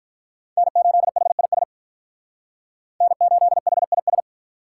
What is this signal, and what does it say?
Morse code, heard as a keyed tone, about 45 wpm, 700 Hz. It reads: N8HIS N8HIS